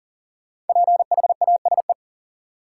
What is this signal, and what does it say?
Morse code, heard as a keyed tone, about 40 wpm, 700 Hz. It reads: PHASE
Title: Sentences from Top 1200 Words - New Word Drill Alt - 40wpm